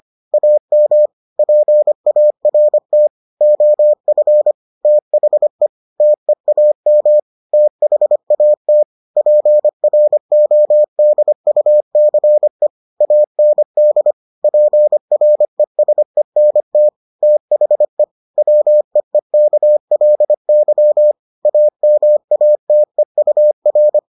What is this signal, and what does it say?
Morse code, heard as a keyed tone, about 25 wpm, 600 Hz. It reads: AM PART OF THE TEAM THAT PRODUCE AND PRESENT THE WEEKLY AMATEUR